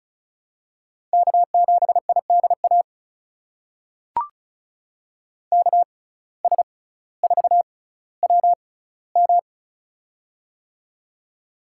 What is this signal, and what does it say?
Morse code, heard as a keyed tone, about 36 wpm, spaced as for 18 wpm, 700 Hz. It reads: K7IDA KS4WM
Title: USA Call Signs - 5 Characters - 6x Character Spacing - 35wpm